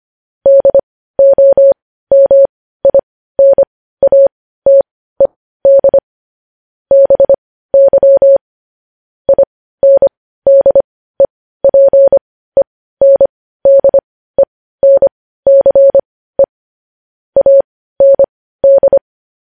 Morse code, heard as a keyed tone, about 25 wpm, spaced as for 15 wpm, 550 Hz. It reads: DOMINATED BY INDEPENDENCE AND
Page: news-headlines-in-morse-code-at-15-wpm